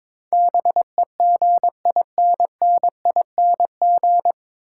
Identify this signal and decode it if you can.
Morse code, heard as a keyed tone, about 22 wpm, 700 Hz. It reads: BEGINNING
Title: Sentences from Top 800 Words - New Word Drill - 22wpm